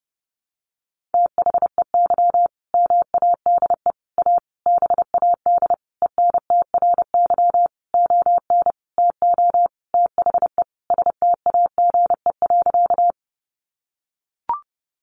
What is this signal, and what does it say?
Morse code, heard as a keyed tone, about 30 wpm, 700 Hz. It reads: THEY MADE A BAD ENTRY ON TO THE STAGE.
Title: Sentences from Top 1500 Words 30wpm